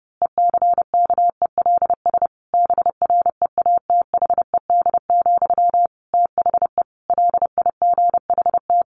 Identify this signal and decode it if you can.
Morse code, heard as a keyed tone, about 28 wpm, 700 Hz. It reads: ECKELS BREATHED, THE LIGHT
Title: A Sound Of Thunder 30wpm